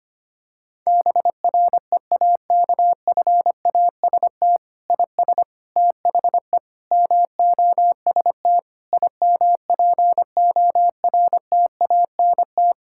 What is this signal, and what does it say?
Morse code, heard as a keyed tone, about 25 wpm, 700 Hz. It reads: BREAKFAST IS THE MOST IMPORTANT
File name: practice / 25wpm / Difficult Sentences with Hints - 25wpm